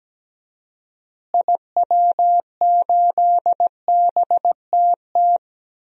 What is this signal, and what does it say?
Morse code, heard as a keyed tone, about 17 wpm, 700 Hz. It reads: IW8BTT
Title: DXCC-100 Call Signs - 6 Characters - Repeated 3x - 17wpm